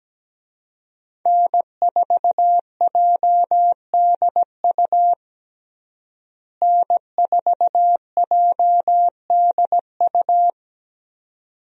Morse code, heard as a keyed tone, about 17 wpm, 700 Hz. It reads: N4JDU N4JDU
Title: USA Call Signs - 5 Characters - Repeated 2x - 17wpm